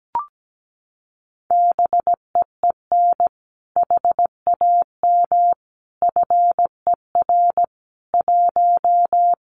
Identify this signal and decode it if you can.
Morse code, heard as a keyed tone, about 17 wpm, 700 Hz. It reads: BEEN HAM FER 1
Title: Common QSO Phrases Encore 17wpm